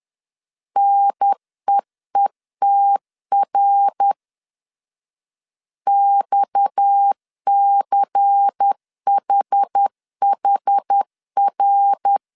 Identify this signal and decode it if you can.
Morse code, heard as a keyed tone, about 10 wpm, 800 Hz. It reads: NEETR XCHHR
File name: CW / DET / 1 6 1500C S A 10wpm 4141